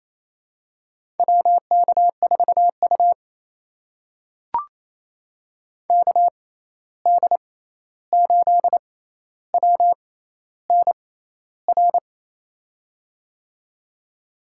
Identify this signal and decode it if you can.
Morse code, heard as a keyed tone, about 28 wpm, spaced as for 13 wpm, 700 Hz. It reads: WK4U KD8WNR
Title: USA Call Signs - 6x Character Spacing - 28wpm